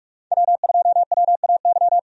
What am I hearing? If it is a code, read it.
Morse code, heard as a keyed tone, about 47 wpm, 700 Hz. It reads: W2WAY
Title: USA Call Signs - 5 Characters - 45wpm